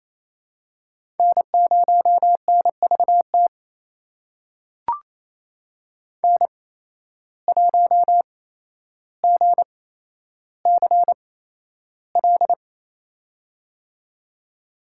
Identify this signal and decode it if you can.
Morse code, heard as a keyed tone, about 29 wpm, spaced as for 12 wpm, 700 Hz. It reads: N0NVT N1GCL
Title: USA Call Signs - 8x Character Spacing - 28wpm